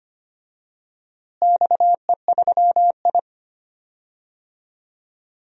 Morse code, heard as a keyed tone, about 25 wpm, 700 Hz. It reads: XE3I
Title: DXCC-100 Call Signs - 4 Characters - Repeated 3x - 25wpm